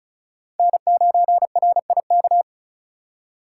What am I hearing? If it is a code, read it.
Morse code, heard as a keyed tone, about 36 wpm, 700 Hz. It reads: N9RIK
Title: USA Call Signs - 6x Character Spacing - 35wpm